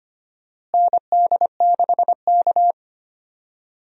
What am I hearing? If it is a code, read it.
Morse code, heard as a keyed tone, about 25 wpm, 700 Hz. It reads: ND6K
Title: USA Call Signs - 4 Characters - Repeated 3x - 25wpm